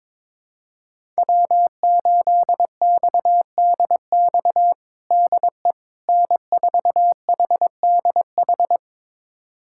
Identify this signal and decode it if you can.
Morse code, heard as a keyed tone, about 22 wpm, 700 Hz. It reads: W8XDX DE N4HDH
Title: USA Callsign DE Callsign - 5 Characters - 22wpm